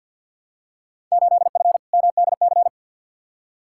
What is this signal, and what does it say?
Morse code, heard as a keyed tone, about 52 wpm, 700 Hz. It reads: 8F MDC
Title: Field Day Exchanges Encore - 50wpm